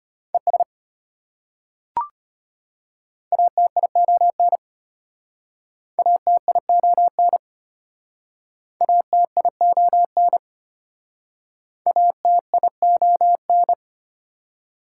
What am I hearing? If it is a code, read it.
Morse code, heard as a keyed tone, about 33 wpm, 700 Hz. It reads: ES ATION ATION ATION ATION